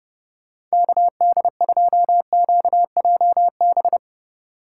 Morse code, heard as a keyed tone, about 37 wpm, 700 Hz. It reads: KD2QJB
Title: Speed Racing - USA 2x3 Call Signs 30wpm